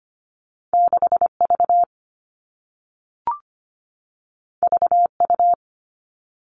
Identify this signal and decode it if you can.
Morse code, heard as a keyed tone, about 25 wpm, 700 Hz. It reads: 6 V V U
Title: All Cut Numbers - 2-Digit 25wpm